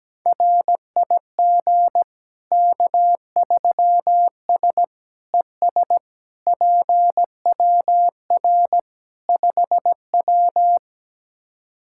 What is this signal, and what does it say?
Morse code, heard as a keyed tone, about 17 wpm, 700 Hz. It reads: RIG K3S ES PWR 5W
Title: K1USN SST QSO Practice 17wpm